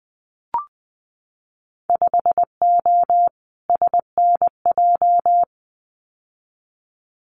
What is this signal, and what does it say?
Morse code, heard as a keyed tone, about 20 wpm, 700 Hz. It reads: 5O SNJ
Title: Winter Field Day Exchanges Encore 20wpm